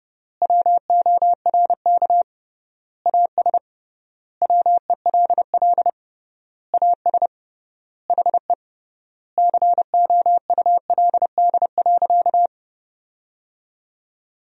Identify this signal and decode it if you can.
Morse code, heard as a keyed tone, about 32 wpm, 700 Hz. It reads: WORK AS WELL AS HE COULD.
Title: 3x Word-Spacing - Sentences from Top 500 Words - 30wpm